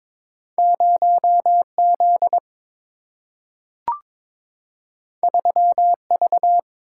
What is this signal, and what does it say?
Morse code, heard as a keyed tone, about 22 wpm, 700 Hz. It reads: 0 Z 3 V
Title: Single-Letter Single-Number 22wpm